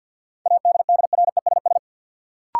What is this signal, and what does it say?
Morse code, heard as a keyed tone, about 52 wpm, 700 Hz. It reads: ADDRESS
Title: Sentences from Top 1000 Words - New Word Drill - 50wpm